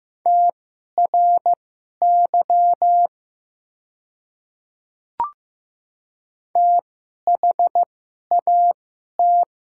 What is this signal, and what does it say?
Morse code, heard as a keyed tone, about 15 wpm, spaced as for 11 wpm, 700 Hz. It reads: TRY THAT
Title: 2x Character Spacing - Top 200 Words - 15wpm